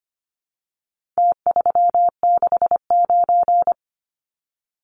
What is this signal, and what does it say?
Morse code, heard as a keyed tone, about 25 wpm, 700 Hz. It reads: T369
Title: Numbers - Contest Style - 4-Digit 25wpm